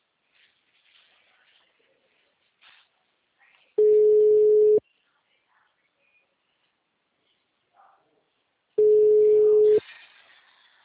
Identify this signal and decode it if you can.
Ringback tone (Europe, 425 Hz).